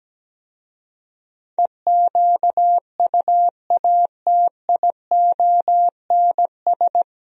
Morse code, heard as a keyed tone, about 17 wpm, 700 Hz. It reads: EQUATIONS
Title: Speed-Racing - Sets of 5 Words 17wpm